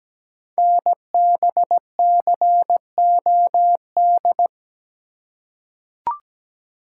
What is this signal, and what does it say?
Morse code, heard as a keyed tone, about 17 wpm, 700 Hz. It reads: NBCOD
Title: Code Groups - 5 Letters - 17wpm